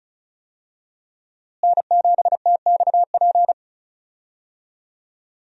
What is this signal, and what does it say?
Morse code, heard as a keyed tone, about 34 wpm, 700 Hz. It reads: N7TXP
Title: USA Call Signs - 5 Characters - Repeated 2x - 35wpm